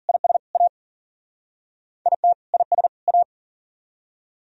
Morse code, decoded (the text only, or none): IS A IT IS A